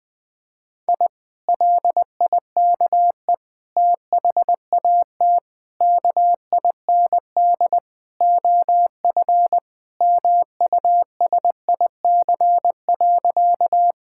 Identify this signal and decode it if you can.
Morse code, heard as a keyed tone, about 20 wpm, 700 Hz. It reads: I LIKE THAT KIND OF MUSIC.
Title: Sentences with 3-Word Phrases - Spoken Beginning - 20wpm